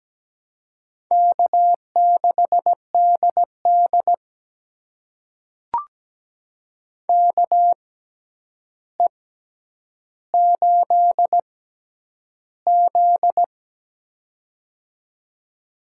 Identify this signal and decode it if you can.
Morse code, heard as a keyed tone, about 17 wpm, spaced as for 9 wpm, 700 Hz. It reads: K6DD KE8Z
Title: USA Call Signs - 6x Character Spacing - 17wpm